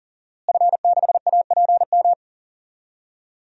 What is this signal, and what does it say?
Morse code, heard as a keyed tone, about 40 wpm, 700 Hz. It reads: F6APM
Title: DXCC-100 Call Signs - 5 Characters - Repeated 3x - 40wpm